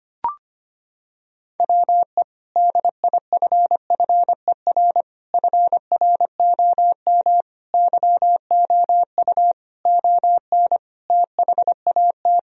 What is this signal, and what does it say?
Morse code, heard as a keyed tone, about 25 wpm, 700 Hz. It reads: WE DIFFER FROM YOU ON THAT
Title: Sentences from Top 1000 Words - Review Part 1 - 25wpm